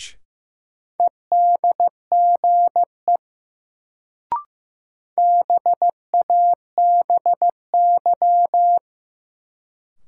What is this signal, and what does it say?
Morse code, heard as a keyed tone, about 15 wpm, 700 Hz. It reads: EDGE BABY
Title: Sentences from Top 1300 Words - New Word Drill - 15wpm